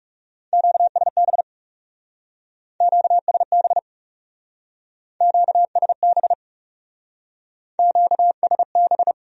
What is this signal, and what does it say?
Morse code, heard as a keyed tone, about 39 wpm, 700 Hz. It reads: QSB QSB QSB QSB